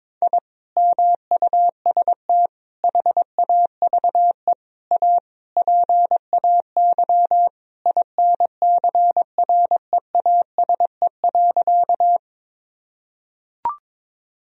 Morse code, heard as a keyed tone, about 22 wpm, 700 Hz. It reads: I MUST HAVE A PAY INCREASE.
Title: Sentences from Top 1000 Words - Review Part 4 - 22wpm